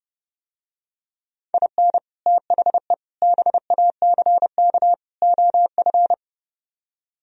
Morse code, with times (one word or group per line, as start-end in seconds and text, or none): 1.5-2.0 IN
2.3-2.9 THE
3.2-4.9 BACK
5.2-6.1 OF